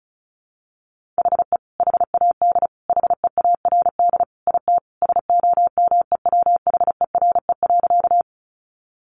Morse code, heard as a keyed tone, about 34 wpm, 700 Hz. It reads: HE HAD HEARD IT SOMEWHERE.
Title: Sentences from Top 1400 Words 35wpm